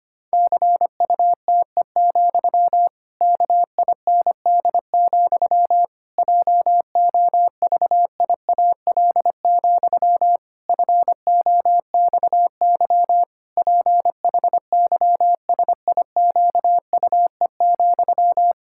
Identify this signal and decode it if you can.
Morse code, heard as a keyed tone, about 25 wpm, 700 Hz. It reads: CUTE, KIND, JOVIAL, FOXY PHYSIQUE,
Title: Pangram Challenge v2 - 25wpm